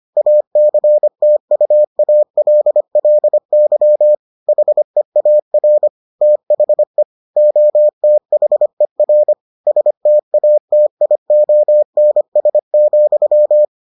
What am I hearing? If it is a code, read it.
Morse code, heard as a keyed tone, about 25 wpm, 600 Hz. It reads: ACTUALLY HEAR THE OTHER STATIONS,